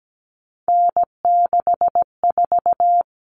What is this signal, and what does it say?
Morse code, heard as a keyed tone, about 17 wpm, 700 Hz. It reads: N64
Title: Common Cut Numbers - 3-Digit 17wpm